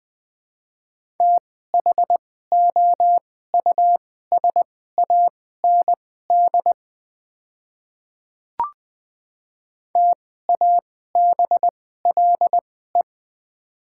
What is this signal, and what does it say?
Morse code, heard as a keyed tone, about 20 wpm, spaced as for 14 wpm, 700 Hz. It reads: THOUSAND TABLE
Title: 2x Character Spacing - Top 500 Words - 20wpm